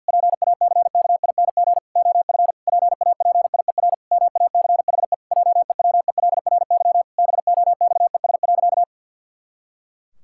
Morse code, heard as a keyed tone, about 50 wpm, 700 Hz. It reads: PACKING OF PAPIER MACHE JEWELRY BOXES.
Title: Pangram Challenge v2 - 50wpm